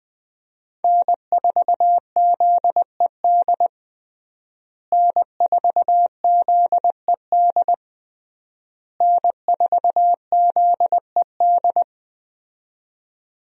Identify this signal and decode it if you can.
Morse code, heard as a keyed tone, about 21 wpm, 700 Hz. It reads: N4ZED N4ZED N4ZED
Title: USA Call Signs - 5 Characters - Repeated 3x - 20wpm